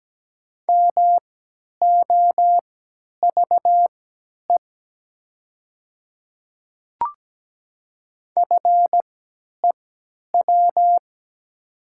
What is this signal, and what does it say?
Morse code, heard as a keyed tone, about 17 wpm, spaced as for 10 wpm, 700 Hz. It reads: MOVE FEW
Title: 3x Character Spacing - Top 300 Words - 17wpm